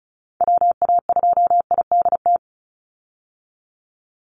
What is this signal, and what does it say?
Morse code, heard as a keyed tone, about 34 wpm, 700 Hz. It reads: WA2IDT
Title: USA Call Signs 35wpm